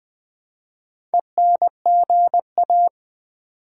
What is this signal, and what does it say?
Morse code, heard as a keyed tone, about 20 wpm, 700 Hz. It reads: ENGA
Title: Build A Word - 4 Letters At A Time - 10-Letter Words - 20wpm